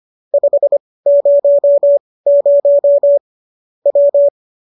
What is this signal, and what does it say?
Morse code, heard as a keyed tone, about 25 wpm, spaced as for 18 wpm, 570 Hz. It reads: 500 W
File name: qso201 18